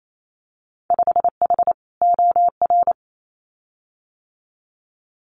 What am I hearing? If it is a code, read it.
Morse code, heard as a keyed tone, about 28 wpm, 700 Hz. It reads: 5H OR
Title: Winter Field Day Exchanges Encore 28wpm